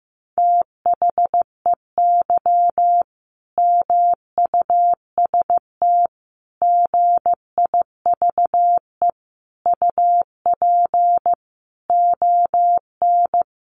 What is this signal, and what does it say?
Morse code, heard as a keyed tone, about 15 wpm, 700 Hz. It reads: THEY MUST GIVE UP ON
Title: Sentences from Top 1100 Words 15wpm